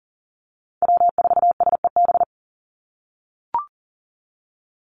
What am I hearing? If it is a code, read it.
Morse code, heard as a keyed tone, about 40 wpm, 700 Hz. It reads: W4SEB
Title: USA Call Signs 40wpm